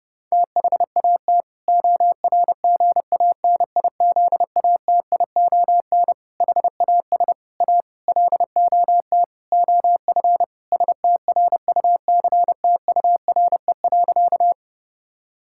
Morse code, heard as a keyed tone, about 32 wpm, 700 Hz. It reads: THAT ORGANIZATION HAS A LOT OF STRUCTURE.
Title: Sentences from Top 1000 Words - Review Part 6 - 30wpm